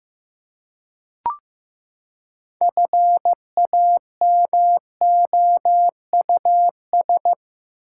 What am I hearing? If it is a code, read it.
Morse code, heard as a keyed tone, about 15 wpm, 700 Hz. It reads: FAMOUS
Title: Six Letter Words - 15wpm